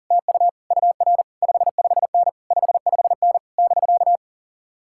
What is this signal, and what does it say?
Morse code, heard as a keyed tone, about 40 wpm, 700 Hz. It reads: TU UR 55N 55N <BK>
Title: Speed-Racing Common QSO Phrases - 40wpm